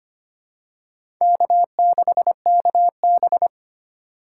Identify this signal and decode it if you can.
Morse code, heard as a keyed tone, about 25 wpm, 700 Hz. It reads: K6KB
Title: USA Call Signs - 8x Character Spacing - 25wpm